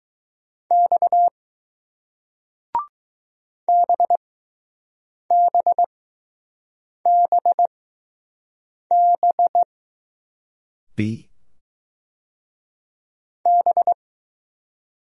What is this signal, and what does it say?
Morse code, heard as a keyed tone, about 20 wpm, spaced as for 6 wpm, 700 Hz. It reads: X BBBB B